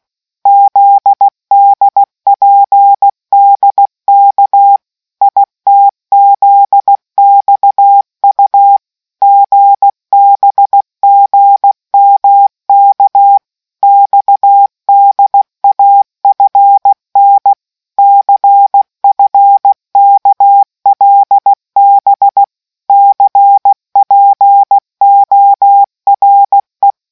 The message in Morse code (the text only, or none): ZDPDK ITZXU GBGMK XDAFN CFKLB CPORE